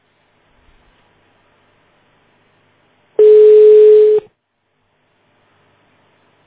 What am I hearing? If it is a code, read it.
Ringback tone (Europe, 425 Hz).
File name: 1010. .38dfcd88-7ad2-47dd-aa6d